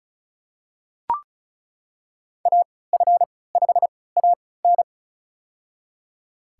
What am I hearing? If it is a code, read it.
Morse code, heard as a keyed tone, about 36 wpm, spaced as for 20 wpm, 700 Hz. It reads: AF5AN
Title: USA Call Signs - 5 Characters - 3x Character Spacing - 35wpm